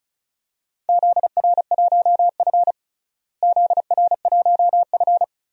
Morse code, heard as a keyed tone, about 36 wpm, 700 Hz. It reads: ZR1F ZR1F
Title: DXCC-100 Call Signs - 4 Characters - Repeated 2x - 35wpm